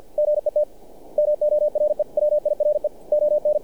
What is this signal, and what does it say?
Morse code, heard as a keyed tone, about 53 wpm, 600 Hz. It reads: GET MORE WIRE ON
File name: archived / CRSnet